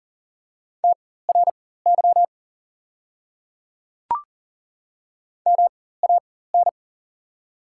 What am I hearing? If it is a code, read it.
Morse code, heard as a keyed tone, about 40 wpm, spaced as for 19 wpm, 700 Hz. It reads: TRY MAN
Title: 4x Character Spacing - Top 200 Words - 40wpm